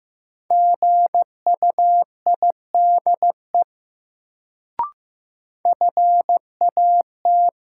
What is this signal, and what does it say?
Morse code, heard as a keyed tone, about 15 wpm, 700 Hz. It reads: GUIDE FAT
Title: Sentences from Top 600 Words - New Word Drill - 15wpm